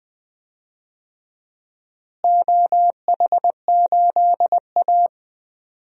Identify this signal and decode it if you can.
Morse code, heard as a keyed tone, about 21 wpm, 700 Hz. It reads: OH8A
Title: DXCC-100 Call Signs - 4 Characters - Repeated 2x - 20wpm